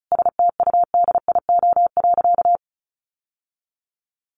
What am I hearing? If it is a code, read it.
Morse code, heard as a keyed tone, about 34 wpm, 700 Hz. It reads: STUDIO.
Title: Sentences from Top 1200 Words 35wpm